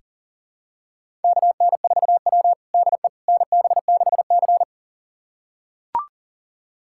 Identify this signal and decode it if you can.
Morse code, heard as a keyed tone, about 40 wpm, 690 Hz. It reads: KN4W DE NB6C
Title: USA Callsign DE Callsign - 4 Characters - 40wpm